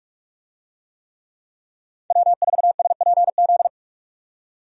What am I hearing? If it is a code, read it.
Morse code, heard as a keyed tone, about 46 wpm, 700 Hz. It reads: W4SPZ
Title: USA Call Signs - 5 Characters - 45wpm